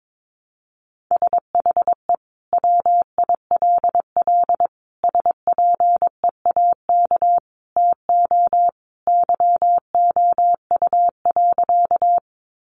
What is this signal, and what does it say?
Morse code, heard as a keyed tone, about 22 wpm, 700 Hz. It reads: SHE WILL SPEAK TO YOU.
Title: Sentences from Top 1100 Words 22wpm